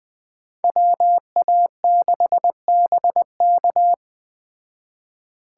Morse code, heard as a keyed tone, about 20 wpm, 700 Hz. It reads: WA6BK
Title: Speed Racing - USA 2x2 Call Signs 17wpm